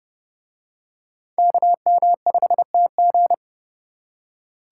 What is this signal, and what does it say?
Morse code, heard as a keyed tone, about 27 wpm, 700 Hz. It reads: KM5TG